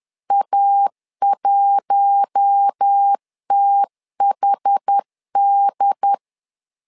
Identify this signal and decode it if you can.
Morse code, heard as a keyed tone, about 10 wpm, 800 Hz. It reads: A1THD
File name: LEZIONI / 4 Q1 10wpm 2448